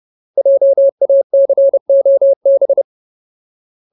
Morse code, heard as a keyed tone, about 30 wpm, 550 Hz. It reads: JACOB
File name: Ultimate Word List / 30WPM Ultimate